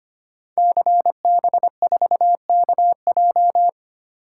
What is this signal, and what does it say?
Morse code, heard as a keyed tone, about 25 wpm, 700 Hz. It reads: CB4KJ
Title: Code Groups - 5 Letter:Numbers - 25wpm